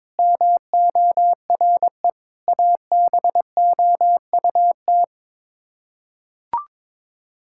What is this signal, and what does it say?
Morse code, heard as a keyed tone, about 22 wpm, 700 Hz. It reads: MORE ABOUT
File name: practice / 22wpm / Sets of 7 Words - 22wpm